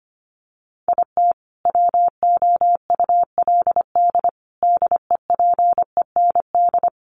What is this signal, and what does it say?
Morse code, heard as a keyed tone, about 25 wpm, 700 Hz. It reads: IT WOULD DEPEND
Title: Sentences from Top 1300 Words 25wpm